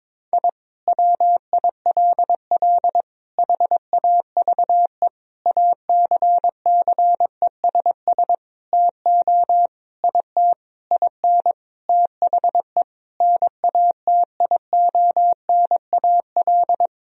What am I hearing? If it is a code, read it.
Morse code, heard as a keyed tone, about 22 wpm, 700 Hz. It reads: I WILL HAVE ACCESS TO IT IN THE NATIONAL
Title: Sentences from Top 1000 Words - Review Part 4 - 22wpm